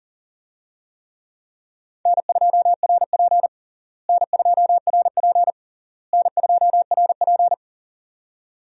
Morse code, heard as a keyed tone, about 37 wpm, 700 Hz. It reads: N2RP N2RP N2RP
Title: USA Call Signs - 4 Characters - Repeated 3x - 40wpm